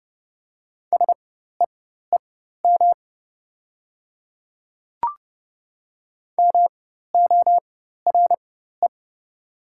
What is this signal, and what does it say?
Morse code, heard as a keyed tone, about 30 wpm, spaced as for 14 wpm, 700 Hz. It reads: SEEM MORE
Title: 4x Character Spacing - Top 300 Words - 30wpm